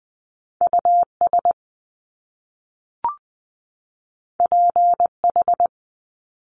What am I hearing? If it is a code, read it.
Morse code, heard as a keyed tone, about 20 wpm, 700 Hz. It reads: US PH
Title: Top Two Letter Combinations in Words 20wpm